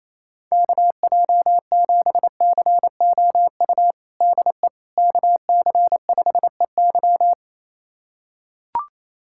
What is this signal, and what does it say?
Morse code, heard as a keyed tone, about 28 wpm, 700 Hz. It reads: KJ7COU DE KC5EY
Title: USA Callsign DE Callsign 28wpm